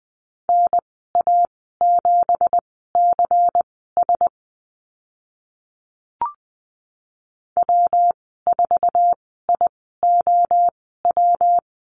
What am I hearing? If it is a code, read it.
Morse code, heard as a keyed tone, about 20 wpm, spaced as for 15 wpm, 700 Hz. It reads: NA7CS W4IOW
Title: 179 Lesson 29g - Slash - Call Signs 20wpm